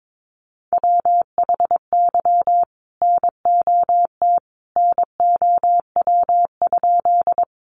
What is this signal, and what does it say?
Morse code, heard as a keyed tone, about 22 wpm, 700 Hz. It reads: WHY NOT NOW?
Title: Sentences from Top 600 Words 22wpm